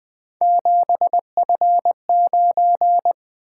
Morse code, heard as a keyed tone, about 20 wpm, 700 Hz. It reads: 7F9
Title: Three Letter-Number Combinations - 20wpm